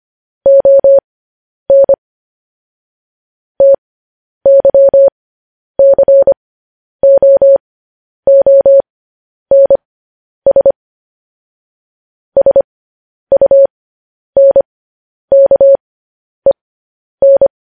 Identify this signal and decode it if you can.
Morse code, heard as a keyed tone, about 25 wpm, spaced as for 10 wpm, 550 Hz. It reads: ON TYCOONS SUNKEN
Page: news-headlines-in-morse-code-at-10-wpm